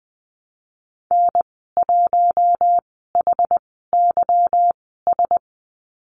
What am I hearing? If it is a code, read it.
Morse code, heard as a keyed tone, about 20 wpm, spaced as for 14 wpm, 700 Hz. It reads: N1HYS
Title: 161 Lesson 26g - Letter g - Call Signs 20wpm